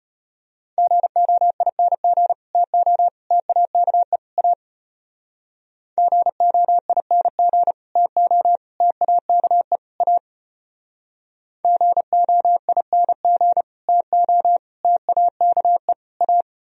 Morse code, heard as a keyed tone, about 30 wpm, 700 Hz. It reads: GOING TO TAKE A GOING TO TAKE A GOING TO TAKE A
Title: Speed-Racing - Sets of 4 Words 25wpm